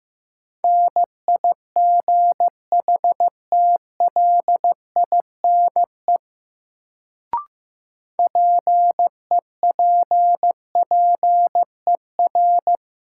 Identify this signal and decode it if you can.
Morse code, heard as a keyed tone, about 15 wpm, 700 Hz. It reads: NIGHTLINE PEPPER
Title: Sets of 5 Words - New Word Drill - 15wpm